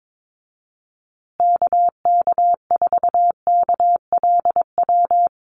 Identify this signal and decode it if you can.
Morse code, heard as a keyed tone, about 22 wpm, 700 Hz. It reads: KK4KLW
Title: USA Call Signs Repeated 3x 22wpm